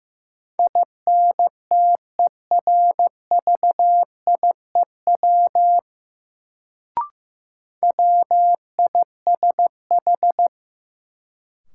Morse code, heard as a keyed tone, about 15 wpm, 700 Hz. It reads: INTERVIEW WISH
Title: Sentences from Top 1600 Words - New Word Drill Alt - 15wpm